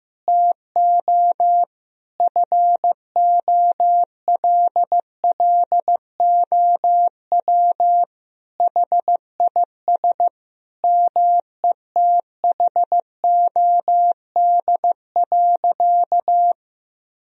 Morse code, heard as a keyed tone, about 16 wpm, 700 Hz. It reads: TO FOLLOW HIS METHOD.
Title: Sentences from Top 1000 Words - Review Part 3 - 15wpm